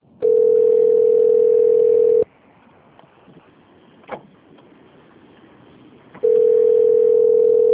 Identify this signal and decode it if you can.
Ringback tone (North America), 440 + 480 Hz.